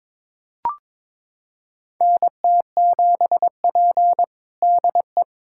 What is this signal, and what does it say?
Morse code, heard as a keyed tone, about 22 wpm, 700 Hz. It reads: NT7P DE
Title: USA Callsign DE Callsign - 4 Characters - 22wpm